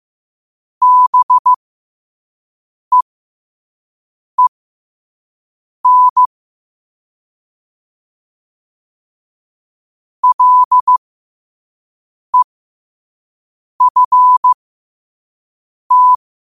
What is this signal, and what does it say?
Morse code, heard as a keyed tone, about 15 wpm, spaced as for 5 wpm, 1000 Hz. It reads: BEEN LEFT
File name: qotdfiles / 5wpm